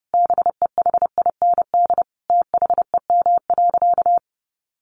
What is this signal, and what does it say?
Morse code, heard as a keyed tone, about 30 wpm, 700 Hz. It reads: BEHIND THEM.
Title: Sentences from Top 1400 Words 30wpm